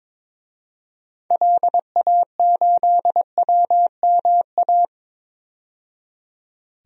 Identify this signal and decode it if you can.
Morse code, heard as a keyed tone, about 22 wpm, 700 Hz. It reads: LA8WMA
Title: DXCC-100 Call Signs - 6 Characters - 22wpm